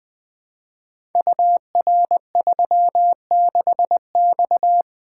Morse code, heard as a keyed tone, about 20 wpm, 700 Hz. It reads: UR36X
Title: Code Groups - 5 Letter:Numbers - 20wpm